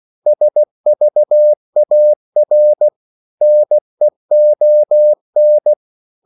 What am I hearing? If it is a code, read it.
Morse code, heard as a keyed tone, about 16 wpm, 600 Hz. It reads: SVAR NEON
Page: Korte danske ord 16wpm